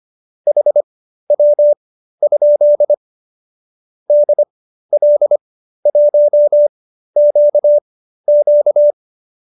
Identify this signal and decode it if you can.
Morse code, heard as a keyed tone, about 25 wpm, spaced as for 13 wpm, 590 Hz. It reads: HW? DL1QQ